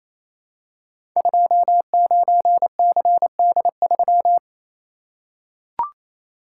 Morse code, heard as a keyed tone, about 27 wpm, 700 Hz. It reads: 29CD3
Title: Code Groups - 5 Letter:Numbers - 28wpm